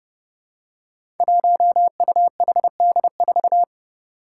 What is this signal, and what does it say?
Morse code, heard as a keyed tone, about 30 wpm, 700 Hz. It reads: 1UHD4
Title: Five Challenging Letter-Number Combinations 30wpm